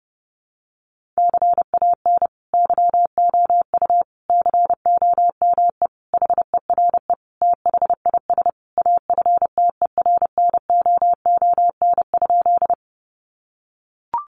Morse code, heard as a keyed tone, about 30 wpm, 700 Hz. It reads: CAN YOU COME HERE THIS AFTERNOON?
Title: Sentences from Top 1500 Words 30wpm